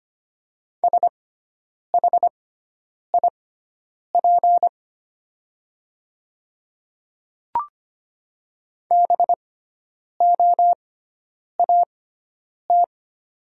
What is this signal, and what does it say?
Morse code, heard as a keyed tone, about 25 wpm, spaced as for 9 wpm, 690 Hz. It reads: SHIP BOAT
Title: 6x Character Spacing - Top 500 Words - 25wpm